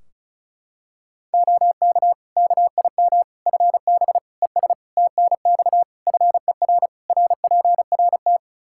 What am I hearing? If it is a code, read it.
Morse code, heard as a keyed tone, about 36 wpm, 700 Hz. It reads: OK KIM FB ES TNX FER RPRT
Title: Practice QSO - Exchange 2 - Info - 35wpm